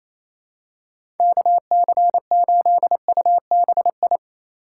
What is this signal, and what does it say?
Morse code, heard as a keyed tone, about 27 wpm, 700 Hz. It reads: KC8UBI
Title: USA Call Signs - 6 Characters - 4x Character Spacing - 28wpm